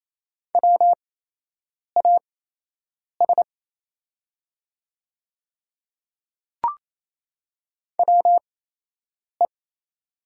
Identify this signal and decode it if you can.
Morse code, heard as a keyed tone, about 28 wpm, spaced as for 8 wpm, 700 Hz. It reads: WAS WE
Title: 8x Character Spacing - Top 100 Words - 28wpm